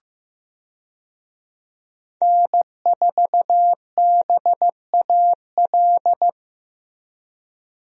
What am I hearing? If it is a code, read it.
Morse code, heard as a keyed tone, about 15 wpm, 700 Hz. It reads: N4BAL
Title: USA Call Signs - 5 Characters - Repeated 3x - 15wpm